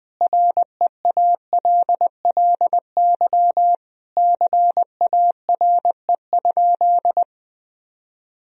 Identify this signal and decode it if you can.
Morse code, heard as a keyed tone, about 20 wpm, 700 Hz. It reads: REALLY CARE?
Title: Sentences from Top 1000 Words - Review Part 3 - 20wpm